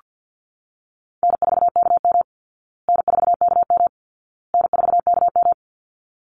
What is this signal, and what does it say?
Morse code, heard as a keyed tone, about 48 wpm, 700 Hz. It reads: N4KM N4KM N4KM
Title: Call Signs Repeated 2x - 50wpm